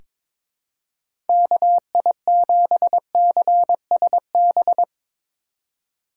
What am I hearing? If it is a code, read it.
Morse code, heard as a keyed tone, about 22 wpm, 700 Hz. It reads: KI7CSB
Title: USA Call Signs - 3x Character Spacing - 22wpm